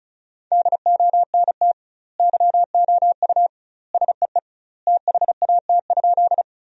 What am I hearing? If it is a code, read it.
Morse code, heard as a keyed tone, about 34 wpm, 700 Hz. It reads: DONT YOU SEE THAT?
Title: 2x Word-Spacing - Sentences from Top 300 Words - 35wpm